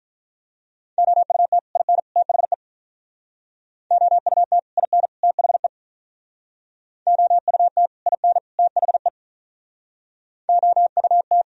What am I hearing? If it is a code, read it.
Morse code, heard as a keyed tone, about 45 wpm, 700 Hz. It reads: OUT IN THE OUT IN THE OUT IN THE OUT